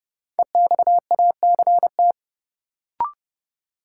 Morse code, heard as a keyed tone, about 32 wpm, 700 Hz. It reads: EXACT
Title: Sentences from Top 1300 Words - New Word Drill Alt - 30wpm